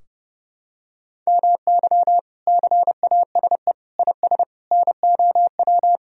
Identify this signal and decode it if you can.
Morse code, heard as a keyed tone, about 30 wpm, 700 Hz. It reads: MY CASE IS NOW
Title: Sentences from Top 1000 Words - Review Part 2 - 30wpm